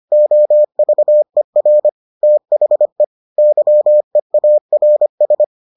Morse code, heard as a keyed tone, about 25 wpm, 600 Hz. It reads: OVER THE YEARS